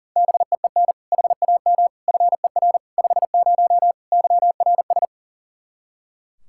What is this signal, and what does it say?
Morse code, heard as a keyed tone, about 37 wpm, 700 Hz. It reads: BEEN HAM FER 50 YRS
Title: K1USN SST QSO Practice 40wpm